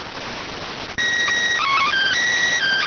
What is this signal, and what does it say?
SSTV (slow-scan television), PD 290.